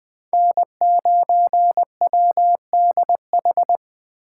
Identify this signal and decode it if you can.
Morse code, heard as a keyed tone, about 21 wpm, 700 Hz. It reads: N9WDH